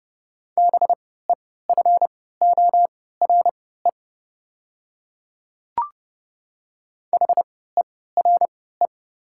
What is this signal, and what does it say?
Morse code, heard as a keyed tone, about 30 wpm, spaced as for 17 wpm, 690 Hz. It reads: BEFORE HERE
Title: 3x Character Spacing - Top 200 Words - 30wpm